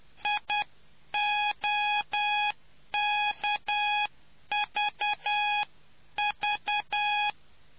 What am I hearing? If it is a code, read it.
Morse code, heard as a keyed tone, about 10 wpm, 830 Hz. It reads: IOKVV